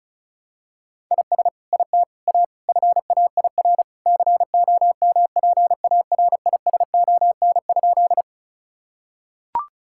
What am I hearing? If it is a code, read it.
Morse code, heard as a keyed tone, about 36 wpm, 700 Hz. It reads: IS IT A FAIR COMPARISON?
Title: Sentences from Top 1600 Words 35wpm